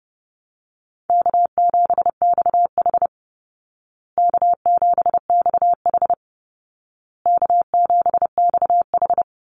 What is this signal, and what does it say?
Morse code, heard as a keyed tone, about 30 wpm, 700 Hz. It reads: K7XH K7XH K7XH
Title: Call Signs Repeated 2x - 30wpm